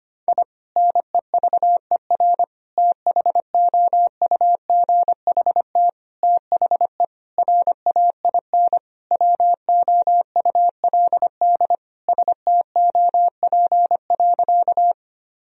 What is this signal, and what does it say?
Morse code, heard as a keyed tone, about 25 wpm, 700 Hz. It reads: I NEVER THOUGHT THE RAIN WOULD STOP.
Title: Sentences from Top 1000 Words - Review Part 1 - 25wpm